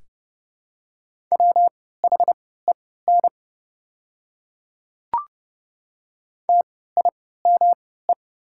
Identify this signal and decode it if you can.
Morse code, heard as a keyed tone, about 30 wpm, spaced as for 17 wpm, 700 Hz. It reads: WHEN TIME
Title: 3x Character Spacing - Top 100 Words - 30wpm